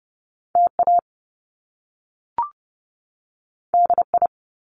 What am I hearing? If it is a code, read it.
Morse code, heard as a keyed tone, about 30 wpm, 700 Hz. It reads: TA DI
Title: Top Two Letter Combinations in Words 30wpm